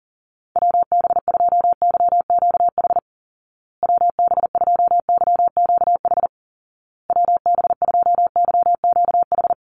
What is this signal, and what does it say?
Morse code, heard as a keyed tone, about 40 wpm, 700 Hz. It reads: WB2YQH WB2YQH WB2YQH
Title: Call Signs Repeated 2x - 40wpm